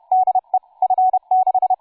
Morse code, heard as a keyed tone, about 31 wpm, 750 Hz, with radio noise. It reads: DE F6
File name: cw forum / tu2 f5ldy 18mhz 16h46